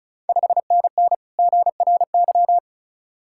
Morse code, heard as a keyed tone, about 34 wpm, 690 Hz. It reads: 5NN GRY
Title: Ontario QSO Party Exchange 35wpm